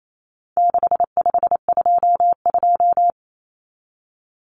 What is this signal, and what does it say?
Morse code, heard as a keyed tone, about 28 wpm, 700 Hz. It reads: 6522
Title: Common Cut Numbers - 4-Digit 28wpm